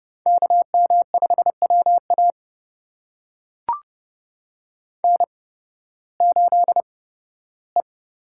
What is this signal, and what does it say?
Morse code, heard as a keyed tone, about 30 wpm, spaced as for 16 wpm, 700 Hz. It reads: KM5WA N8E
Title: USA Call Signs - 5 Characters - 8x Character Spacing - 30wpm